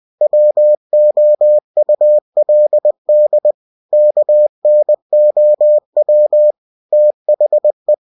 Morse code, heard as a keyed tone, about 20 wpm, 600 Hz. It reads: WOULD KNOW THE
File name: cw-books / winnie-the-pooh / winnie-20wpm-page-0091